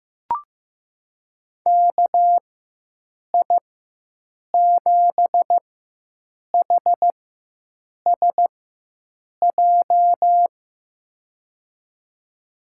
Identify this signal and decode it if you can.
Morse code, heard as a keyed tone, about 15 wpm, spaced as for 7 wpm, 700 Hz. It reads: KI7HSJ
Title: USA Call Signs - 6 Characters - 4x Character Spacing - 15wpm